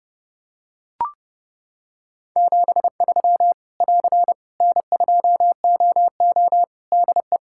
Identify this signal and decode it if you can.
Morse code, heard as a keyed tone, about 28 wpm, 700 Hz. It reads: 73 <AR> N2OO DE
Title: Practice QSO - Exchange 4 - Ending - 30wpm